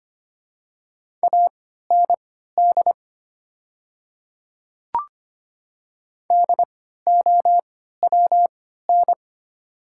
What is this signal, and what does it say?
Morse code, heard as a keyed tone, about 25 wpm, spaced as for 14 wpm, 690 Hz. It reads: AND DOWN
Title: 3x Character Spacing - Top 200 Words - 25wpm